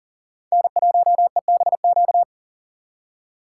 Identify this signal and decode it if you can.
Morse code, heard as a keyed tone, about 51 wpm, 700 Hz. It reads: N1EBQ